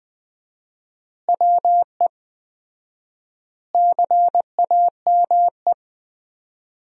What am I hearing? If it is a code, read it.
Morse code, heard as a keyed tone, about 20 wpm, 700 Hz. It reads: WE CAME
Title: 4x Word-Spacing - Sentences from Top 400 Words - 20wpm